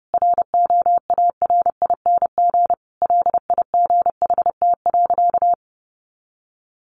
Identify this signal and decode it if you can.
Morse code, heard as a keyed tone, about 30 wpm, 700 Hz. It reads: ROARING LIGHT.
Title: A Sound Of Thunder 30wpm